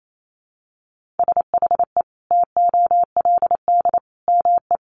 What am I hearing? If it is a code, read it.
Morse code, heard as a keyed tone, about 28 wpm, 700 Hz. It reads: SHE TOLD ME
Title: Sentences from Top 400 Words 28wpm